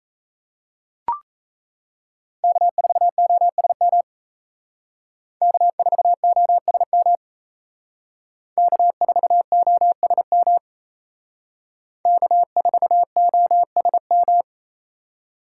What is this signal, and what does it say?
Morse code, heard as a keyed tone, about 35 wpm, 700 Hz. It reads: K4OSM K4OSM K4OSM K4OSM
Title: Speed Racing - USA 1x3 Call Signs 28wpm